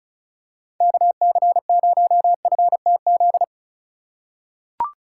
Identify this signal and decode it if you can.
Morse code, heard as a keyed tone, about 36 wpm, 700 Hz. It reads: KC0FTZ
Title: USA Call Signs - 6 Characters - 3x Character Spacing - 35wpm